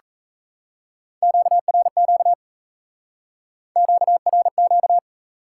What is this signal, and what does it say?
Morse code, heard as a keyed tone, about 36 wpm, 700 Hz. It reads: QRQ QRQ